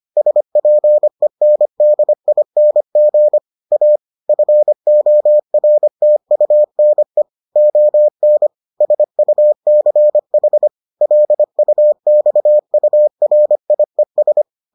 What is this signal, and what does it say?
Morse code, heard as a keyed tone, about 25 wpm, 600 Hz. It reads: SPENDING A FORTUNE ON SUCH LUXURIES